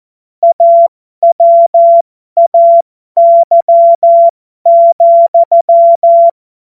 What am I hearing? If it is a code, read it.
Morse code, heard as a keyed tone, about 14 wpm, spaced as for 12 wpm, 680 Hz. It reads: AWAY,